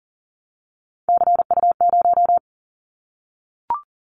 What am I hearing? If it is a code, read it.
Morse code, heard as a keyed tone, about 40 wpm, 690 Hz. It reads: CU0
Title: Callsign Prefixes for Top 100 DXCC - 40wpm